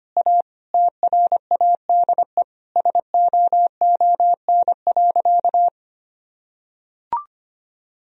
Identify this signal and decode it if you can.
Morse code, heard as a keyed tone, about 25 wpm, 700 Hz. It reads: A TRADE SOON.
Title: Sentences from Top 1000 Words - Review Part 4 - 25wpm